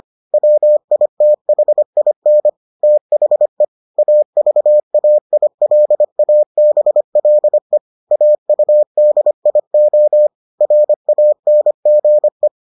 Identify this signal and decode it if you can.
Morse code, heard as a keyed tone, about 25 wpm, 600 Hz. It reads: WITHIN THE AVAILABLE AUDIO RANGE